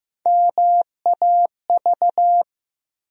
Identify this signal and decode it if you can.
Morse code, heard as a keyed tone, about 15 wpm, 700 Hz. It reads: MAV
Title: USA Call Sign Postfixes - 3 Letters 15wpm